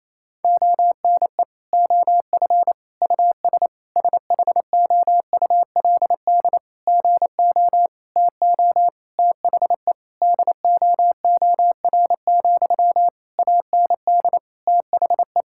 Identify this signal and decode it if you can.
Morse code, heard as a keyed tone, about 28 wpm, 700 Hz. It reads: ONE OF US SHOULD GO TO THE DOOR, AND THE
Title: Sentences from Top 1600 Words 28wpm